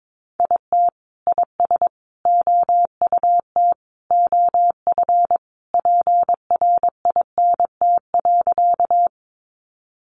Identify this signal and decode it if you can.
Morse code, heard as a keyed tone, about 22 wpm, 700 Hz. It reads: IT IS OUT OF PRINT.
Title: Sentences from Top 1400 Words 22wpm